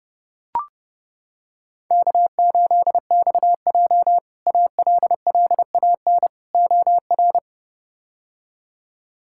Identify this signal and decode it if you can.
Morse code, heard as a keyed tone, about 32 wpm, 700 Hz. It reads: K8XJ ALLAN OR
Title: K1USN SST QSO Practice - 30wpm